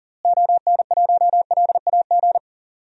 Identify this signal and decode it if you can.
Morse code, heard as a keyed tone, about 40 wpm, 700 Hz. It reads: ON1LAG
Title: DXCC-100 Call Signs - 6 Characters - Repeated 3x - 40wpm